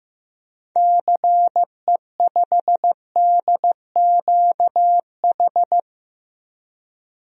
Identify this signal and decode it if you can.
Morse code, heard as a keyed tone, about 15 wpm, 700 Hz. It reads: CE5DQH
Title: DXCC-100 Call Signs - 6 Characters - 15wpm